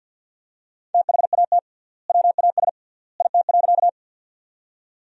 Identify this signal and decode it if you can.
Morse code, heard as a keyed tone, about 50 wpm, 700 Hz. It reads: THAT WAS IT.
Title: 3x Word-Spacing - Sentences from Top 100 Words - 50wpm